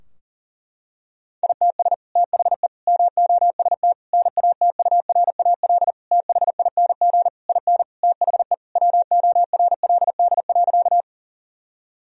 Morse code, heard as a keyed tone, about 40 wpm, 700 Hz. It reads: ITS THE MOST NATURAL THING IN THE WORLD.
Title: Sentences from Top 1000 Words - Review Part 4 - 40wpm